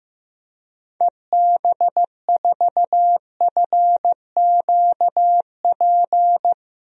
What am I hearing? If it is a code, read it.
Morse code, heard as a keyed tone, about 15 wpm, 700 Hz. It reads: EB4FQP
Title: DXCC-100 Call Signs - 6 Characters - 15wpm